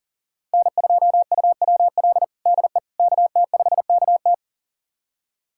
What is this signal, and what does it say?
Morse code, heard as a keyed tone, about 40 wpm, 700 Hz. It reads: N2UWL DE KT5KT
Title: USA Callsign DE Callsign - 5 Characters - 40wpm